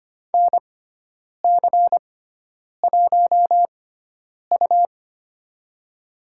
Morse code, heard as a keyed tone, about 25 wpm, spaced as for 9 wpm, 700 Hz. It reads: NC1U
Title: USA Call Signs - 4 Characters - 6x Character Spacing - 25wpm